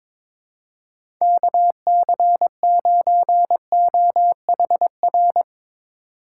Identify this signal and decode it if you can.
Morse code, heard as a keyed tone, about 23 wpm, 700 Hz. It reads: KC9OHR
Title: USA Call Signs - 6 Characters - 8x Character Spacing - 22wpm